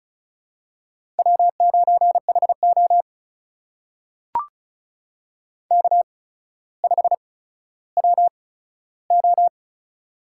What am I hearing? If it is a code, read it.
Morse code, heard as a keyed tone, about 36 wpm, spaced as for 15 wpm, 700 Hz. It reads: W9HO K5WO
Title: USA Call Signs - 4 Characters - 8x Character Spacing - 35wpm